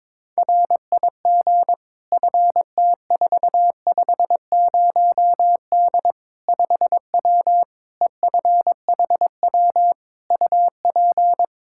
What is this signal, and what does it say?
Morse code, heard as a keyed tone, about 22 wpm, 700 Hz. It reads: RIG FT450D 5W EFHW UP
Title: Practice QSO - Exchange 2 - Info - 22wpm